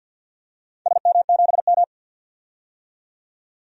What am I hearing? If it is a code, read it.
Morse code, heard as a keyed tone, about 53 wpm, 700 Hz. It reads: SM7M
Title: DXCC-100 Call Signs - 4 Characters - 50wpm